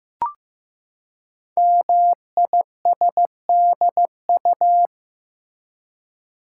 Morse code, heard as a keyed tone, about 15 wpm, 700 Hz. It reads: MISDU
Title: Code Groups - 5 Letters - 15wpm